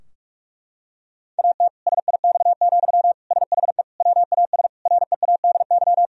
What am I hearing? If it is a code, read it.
Morse code, heard as a keyed tone, about 42 wpm, 700 Hz. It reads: AT SIX, SHE WAS READY
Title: Sentences from Top 1000 Words - Review Part 2 - 45wpm